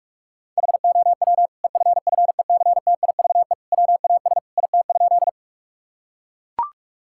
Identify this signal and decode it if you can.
Morse code, heard as a keyed tone, about 41 wpm, 700 Hz. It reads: HOW EFFECTIVE WAS IT?
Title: Sentences from Top 1000 Words - Review Part 6 - 45wpm